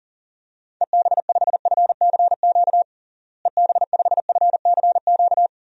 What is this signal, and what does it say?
Morse code, heard as a keyed tone, about 40 wpm, 700 Hz. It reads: EB5FCQ EB5FCQ
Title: DXCC-100 Call Signs - 6 Characters - Repeated 2x - 40wpm